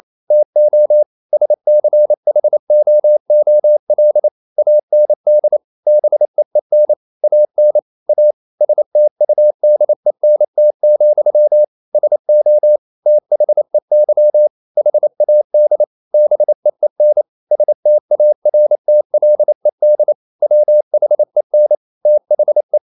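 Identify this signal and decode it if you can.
Morse code, heard as a keyed tone, about 28 wpm, 600 Hz. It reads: TO SCHOOL AND BEEN AN A STUDENT, SO THEY HAD BEEN STARTLED WHEN THE
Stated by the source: Morse code practice at 28 wpm, text of Ken Follett's novel Never in English, from CW med Gnister.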